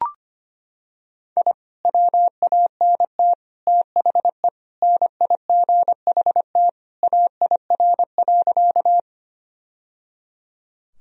Morse code, heard as a keyed tone, about 25 wpm, 700 Hz. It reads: I WANT THE NIGHT AIR.
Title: Sentences from Top 1000 Words - Review Part 1 - 25wpm